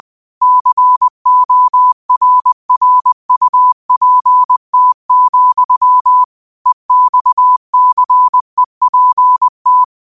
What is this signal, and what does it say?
Morse code, heard as a keyed tone, about 20 wpm, 1000 Hz. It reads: CORRUPT, EXCEPT